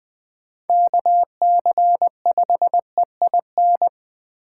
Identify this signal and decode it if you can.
Morse code, heard as a keyed tone, about 21 wpm, 700 Hz. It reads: KC5EIN